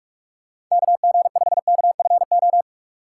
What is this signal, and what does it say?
Morse code, heard as a keyed tone, about 46 wpm, 700 Hz. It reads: KG5KFO
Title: USA Call Signs - 6 Characters - 3x Character Spacing - 45wpm